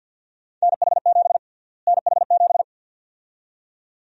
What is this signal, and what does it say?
Morse code, heard as a keyed tone, about 52 wpm, 700 Hz. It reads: NH7 NH7
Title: USA Call Sign Prefixes - Repeated 2x - 50wpm